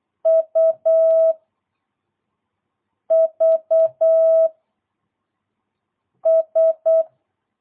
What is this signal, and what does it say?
Morse code, heard as a keyed tone, about 8 wpm, 650 Hz. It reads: U V S
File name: jh8cbh / 2022cw